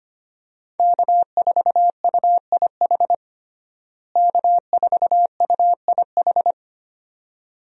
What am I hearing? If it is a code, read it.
Morse code, heard as a keyed tone, about 25 wpm, 700 Hz. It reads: K4UIH K4UIH
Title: USA Call Signs - 5 Characters - Repeated 2x - 25wpm